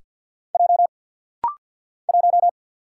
Morse code, heard as a keyed tone, about 48 wpm, 700 Hz. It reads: J 1
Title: Single Letter-Number - Rapid-Fire 50wpm